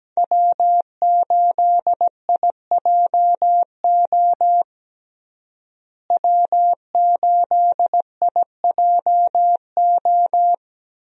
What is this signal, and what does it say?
Morse code, heard as a keyed tone, about 17 wpm, 700 Hz. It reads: W8IJO W8IJO
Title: USA Call Signs - 5 Characters - Repeated 3x - 17wpm